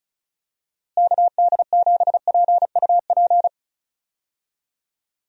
Morse code, heard as a keyed tone, about 34 wpm, 700 Hz. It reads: KD7PUP
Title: USA Call Signs - 6 Characters - Repeated 2x - 35wpm